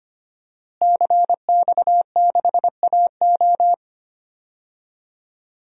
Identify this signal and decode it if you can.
Morse code, heard as a keyed tone, about 25 wpm, 700 Hz. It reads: CX6AO
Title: DXCC-100 Call Signs - 5 Characters - Repeated 3x - 25wpm